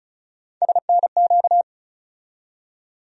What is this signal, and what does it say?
Morse code, heard as a keyed tone, about 36 wpm, 700 Hz. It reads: SNQ
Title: USA Call Sign Postfixes - 3 Letters 35wpm